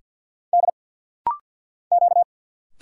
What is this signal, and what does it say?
Morse code, heard as a keyed tone, about 50 wpm, 700 Hz. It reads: D Q
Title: Single Letter-Number Alt - Rapid-Fire 50wpm